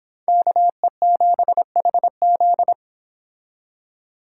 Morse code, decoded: KE7HZ

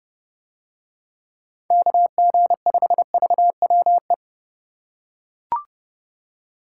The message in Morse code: KG5VWE

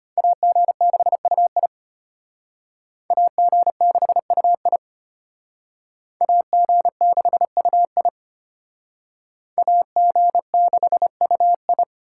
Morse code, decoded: AG6UI AG6UI AG6UI AG6UI

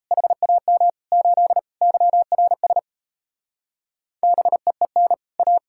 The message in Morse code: HAM 8 YRS BEEN A

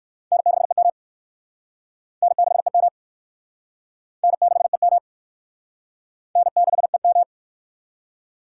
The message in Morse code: N6EM N6EM N6EM N6EM

